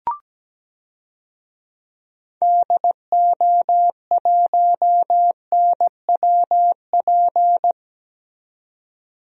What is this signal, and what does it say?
Morse code, heard as a keyed tone, about 17 wpm, 700 Hz. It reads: DO1NWP
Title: DXCC-100 Call Signs - 6 Characters - 17wpm